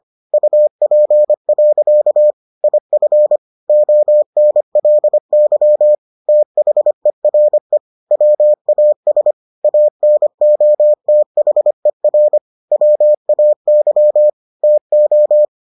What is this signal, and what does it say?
Morse code, heard as a keyed tone, about 25 wpm, 600 Hz. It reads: UP. IF ONLY THERE WAS ANOTHER WAY TO